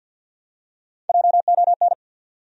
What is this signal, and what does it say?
Morse code, heard as a keyed tone, about 53 wpm, 700 Hz. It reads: JON
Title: USA Common Names - Max 3 Letters - 50wpm